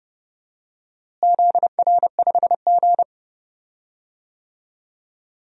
Morse code, heard as a keyed tone, about 30 wpm, 700 Hz. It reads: ZR5G
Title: DXCC-100 Call Signs - 4 Characters - Repeated 2x - 30wpm